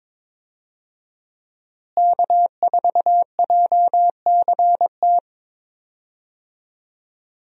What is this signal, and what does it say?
Morse code, heard as a keyed tone, about 22 wpm, 690 Hz. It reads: K4JCT
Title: USA Call Signs - 5 Characters - Repeated 3x - 22wpm